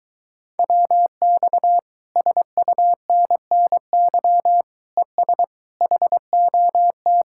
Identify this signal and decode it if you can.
Morse code, heard as a keyed tone, about 24 wpm, 700 Hz. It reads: WX SUNNY ES HOT